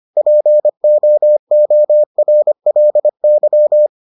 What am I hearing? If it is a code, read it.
Morse code, heard as a keyed tone, about 25 wpm, 600 Hz. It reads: POORLY